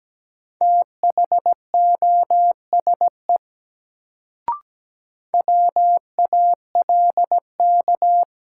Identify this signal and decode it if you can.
Morse code, heard as a keyed tone, about 17 wpm, 700 Hz. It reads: THOSE WALK
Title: Sentences from Top 300 Words - New Word Drill - 17wpm